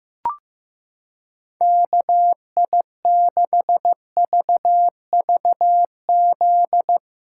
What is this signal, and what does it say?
Morse code, heard as a keyed tone, about 16 wpm, 700 Hz. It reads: KI6VVZ
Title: USA Callsign DE Callsign - 6 Characters - 15wpm